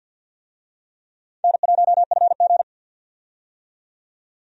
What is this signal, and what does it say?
Morse code, heard as a keyed tone, about 52 wpm, 700 Hz. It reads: N1FG
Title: USA Call Signs - 4 Characters - 50wpm